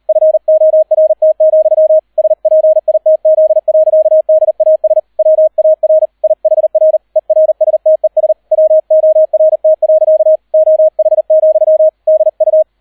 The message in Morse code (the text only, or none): FORT, SPITZ. DAS WAR IHR ERSTES WORT. OH, DU